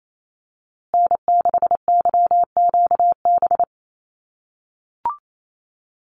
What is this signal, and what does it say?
Morse code, heard as a keyed tone, about 28 wpm, 700 Hz. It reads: N6YQB
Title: USA Call Signs Repeated 2x 28wpm